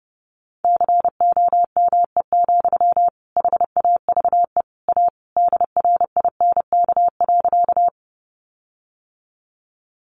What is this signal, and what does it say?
Morse code, heard as a keyed tone, about 30 wpm, 700 Hz. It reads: COME, HAVE A DRINK.
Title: Sentences from Top 1500 Words 30wpm